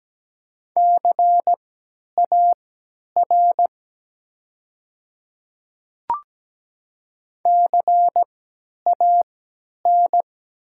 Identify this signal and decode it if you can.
Morse code, heard as a keyed tone, about 17 wpm, spaced as for 10 wpm, 700 Hz. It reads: CAR CAN
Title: 3x Character Spacing - Top 300 Words - 17wpm